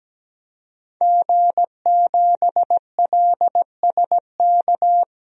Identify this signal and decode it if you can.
Morse code, heard as a keyed tone, about 17 wpm, 700 Hz. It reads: G7LSK
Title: DXCC-100 Call Signs - 5 Characters - Repeated 2x - 17wpm